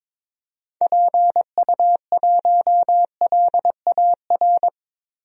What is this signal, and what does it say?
Morse code, heard as a keyed tone, about 22 wpm, 700 Hz. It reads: PU1LAR
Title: DXCC-100 Call Signs - 6 Characters - Repeated 3x - 22wpm